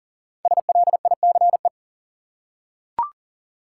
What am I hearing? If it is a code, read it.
Morse code, heard as a keyed tone, about 40 wpm, 700 Hz. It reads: SLICE
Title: Five Letter Words - 40wpm